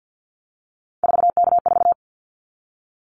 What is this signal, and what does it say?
Morse code, heard as a keyed tone, about 50 wpm, 700 Hz. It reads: 4K4
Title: Callsign Prefixes for Top 100 DXCC - 50wpm